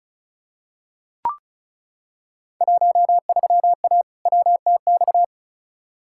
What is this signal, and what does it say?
Morse code, heard as a keyed tone, about 34 wpm, 700 Hz. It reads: 13A WTX
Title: Field Day Exchanges Encore - 35wpm